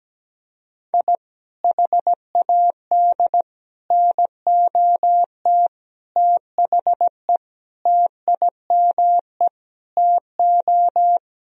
Morse code, heard as a keyed tone, about 17 wpm, 700 Hz. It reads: I HAD NOT THE TIME TO
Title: Sentences from Top 1000 Words - Review Part 1 - 17wpm